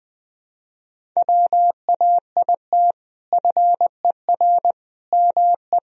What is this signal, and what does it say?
Morse code, heard as a keyed tone, about 20 wpm, 700 Hz. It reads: WAIT FER ME
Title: Sentences from Top 500 Words and Abbreviations 20wpm